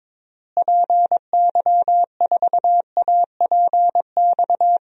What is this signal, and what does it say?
Morse code, heard as a keyed tone, about 22 wpm, 700 Hz. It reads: PY4APX
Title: DXCC-100 Call Signs - 6 Characters - Repeated 3x - 22wpm